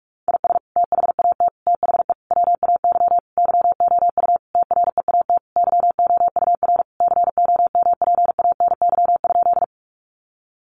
Morse code, heard as a keyed tone, about 46 wpm, 700 Hz. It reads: IS THAT THE WAY YOU TREAT YOUR COMPANY?
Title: Sentences from Top 1300 Words 45wpm